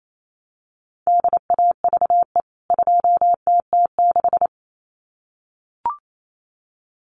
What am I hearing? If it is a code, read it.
Morse code, heard as a keyed tone, about 28 wpm, 700 Hz. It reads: DAVE 2TT6
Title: CWT Exchange 28wpm